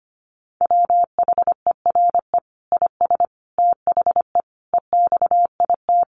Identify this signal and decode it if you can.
Morse code, heard as a keyed tone, about 25 wpm, 700 Hz. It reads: WHERE IS THE EXIT
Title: Sentences from Top 1100 Words 25wpm